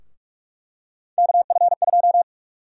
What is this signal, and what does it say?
Morse code, heard as a keyed tone, about 48 wpm, 700 Hz. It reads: KF2
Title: USA Call Sign Prefixes - Repeated 2x - 45wpm